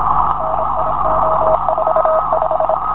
Morse code, heard as a keyed tone, about 25 wpm, 650 Hz, with noise. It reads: NIO45